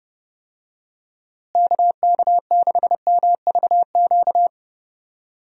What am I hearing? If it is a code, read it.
Morse code, heard as a keyed tone, about 32 wpm, 700 Hz. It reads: KK6MVQ
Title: USA Call Signs - 6 Characters - Repeated 3x - 30wpm